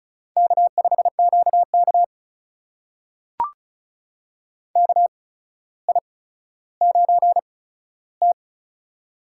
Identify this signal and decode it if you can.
Morse code, heard as a keyed tone, about 36 wpm, spaced as for 15 wpm, 700 Hz. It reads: K5QK KI9T
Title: USA Call Signs - 4 Characters - 8x Character Spacing - 35wpm